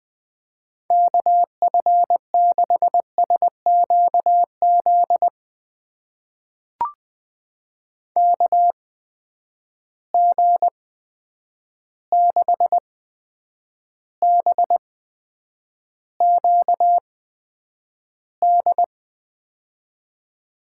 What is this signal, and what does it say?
Morse code, heard as a keyed tone, about 21 wpm, spaced as for 9 wpm, 700 Hz. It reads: KF6SQZ KG6BQD
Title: USA Call Signs - 6 Characters - 8x Character Spacing - 20wpm